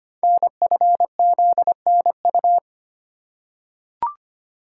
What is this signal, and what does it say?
Morse code, heard as a keyed tone, about 25 wpm, 700 Hz. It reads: NFZNU
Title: Code Groups - 5 Letters - 25wpm